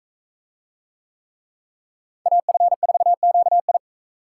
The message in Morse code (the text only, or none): AF4QI